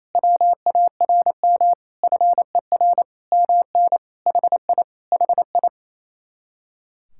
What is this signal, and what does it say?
Morse code, heard as a keyed tone, about 30 wpm, 700 Hz. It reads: WARM FER MN HI HI
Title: Speed-Racing Common QSO Phrases - 28wpm